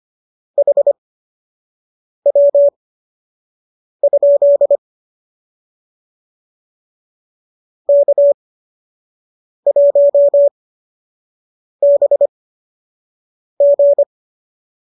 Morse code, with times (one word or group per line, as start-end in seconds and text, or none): 0.6-4.8 HW?
7.9-14.0 K1BG